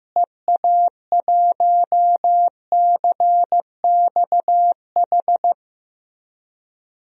Morse code, heard as a keyed tone, about 15 wpm, 700 Hz. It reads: EA1CXH
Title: DXCC-100 Call Signs - 6 Characters - Repeated 3x - 15wpm